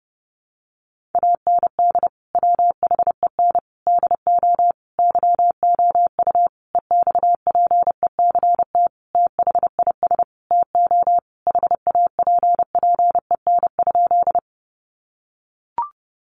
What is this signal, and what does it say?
Morse code, heard as a keyed tone, about 30 wpm, 700 Hz. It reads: AND WHEN DO YOU EXPECT THIS TO HAPPEN?
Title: Sentences from Top 1100 Words 30wpm